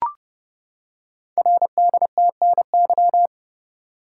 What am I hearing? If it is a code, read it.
Morse code, heard as a keyed tone, about 28 wpm, 700 Hz. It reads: RDTNY
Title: Code Groups - 5 Letters - 30wpm